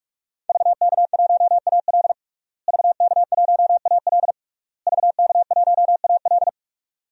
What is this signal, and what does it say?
Morse code, heard as a keyed tone, about 46 wpm, 700 Hz. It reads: VK1AL VK1AL VK1AL
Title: DXCC-100 Call Signs - 5 Characters - Repeated 3x - 45wpm